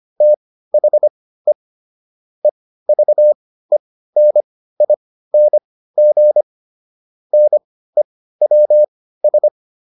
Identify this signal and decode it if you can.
Morse code, heard as a keyed tone, about 25 wpm, spaced as for 15 wpm, 600 Hz. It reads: THE EVENING NEWS